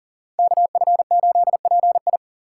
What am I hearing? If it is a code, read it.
Morse code, heard as a keyed tone, about 40 wpm, 700 Hz. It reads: KF8PI
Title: USA Call Signs - 5 Characters - 6x Character Spacing - 40wpm